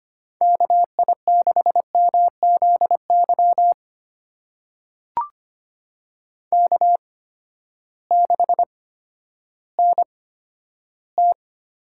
Morse code, heard as a keyed tone, about 25 wpm, spaced as for 12 wpm, 700 Hz. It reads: KI6MZY K6NT
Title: USA Call Signs - 8x Character Spacing - 25wpm